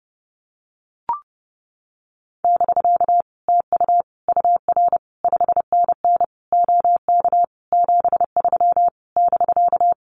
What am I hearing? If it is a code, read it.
Morse code, heard as a keyed tone, about 31 wpm, 700 Hz. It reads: <BK> TU UR 5NN OK 73 <BK>
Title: Common QSO Phrases Encore 30wpm